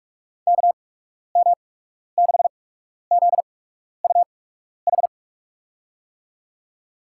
Morse code, heard as a keyed tone, about 46 wpm, spaced as for 12 wpm, 700 Hz. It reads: KM6ZUH
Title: USA Call Signs - 6 Characters - 8x Character Spacing - 45wpm